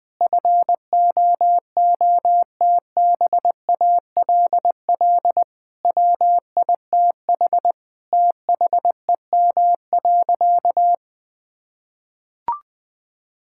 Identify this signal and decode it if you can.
Morse code, heard as a keyed tone, about 20 wpm, 700 Hz. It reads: FOOTBALL WITH THEM.
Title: Sentences from Top 1600 Words 20wpm